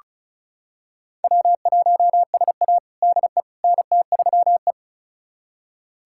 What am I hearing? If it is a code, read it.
Morse code, heard as a keyed tone, about 36 wpm, 700 Hz. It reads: W1SA DE NT3E
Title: USA Callsign DE Callsign - 4 Characters - 35wpm